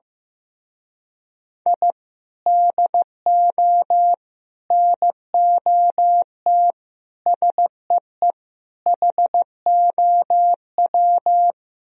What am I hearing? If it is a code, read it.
Morse code, heard as a keyed tone, about 15 wpm, 700 Hz. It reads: I DO NOT SEE HOW
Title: Speed-Racing - Sets of 5 Words 15wpm